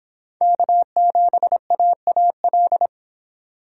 Morse code, heard as a keyed tone, about 29 wpm, 700 Hz. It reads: K7AAL